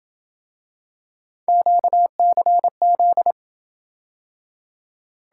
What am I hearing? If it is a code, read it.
Morse code, heard as a keyed tone, about 27 wpm, 700 Hz. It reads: QCZ